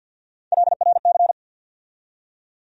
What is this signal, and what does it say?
Morse code, heard as a keyed tone, about 53 wpm, 700 Hz. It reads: LRC
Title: USA Call Sign Postfixes - 3 Letters 50wpm